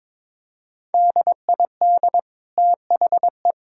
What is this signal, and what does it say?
Morse code, heard as a keyed tone, about 22 wpm, 700 Hz. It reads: DID THE
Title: Sentences from Top 1000 Words - Review Part 6 - 22wpm